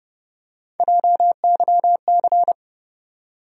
Morse code, heard as a keyed tone, about 30 wpm, 700 Hz. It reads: JYC
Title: USA Call Sign Postfixes - 3 Letters 30wpm